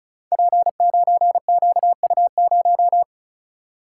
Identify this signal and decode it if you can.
Morse code, heard as a keyed tone, about 33 wpm, 700 Hz. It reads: P9QU0
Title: Code Groups - 5 Letter:Numbers - 35wpm